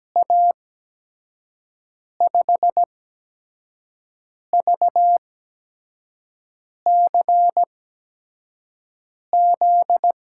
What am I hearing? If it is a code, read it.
Morse code, heard as a keyed tone, about 17 wpm, spaced as for 5 wpm, 700 Hz. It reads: A5VCZ
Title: USA Call Signs - 8x Character Spacing - 17wpm